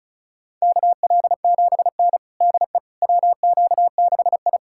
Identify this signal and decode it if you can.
Morse code, heard as a keyed tone, about 36 wpm, 700 Hz. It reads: KL7N DE WQ6I
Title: USA Callsign DE Callsign - 4 Characters - 35wpm